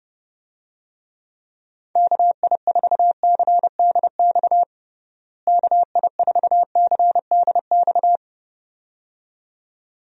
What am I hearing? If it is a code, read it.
Morse code, heard as a keyed tone, about 30 wpm, 700 Hz. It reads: KI4CDX KI4CDX
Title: USA Call Signs - 6 Characters - Repeated 2x - 30wpm